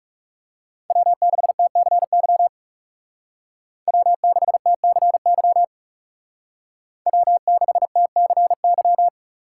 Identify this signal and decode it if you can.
Morse code, heard as a keyed tone, about 36 wpm, 700 Hz. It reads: W6TCY W6TCY W6TCY
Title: Speed Racing - USA 1x3 Call Signs 30wpm